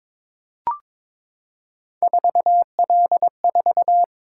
Morse code, heard as a keyed tone, about 22 wpm, 700 Hz. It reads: 4L4
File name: practice / 22wpm / Three Letter-Number Combinations - 22wpm